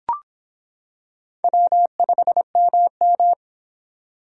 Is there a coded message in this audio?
Yes: Morse code from 1.4 to 3.3 s, heard as a keyed tone, about 26 wpm, 700 Hz. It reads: W5MM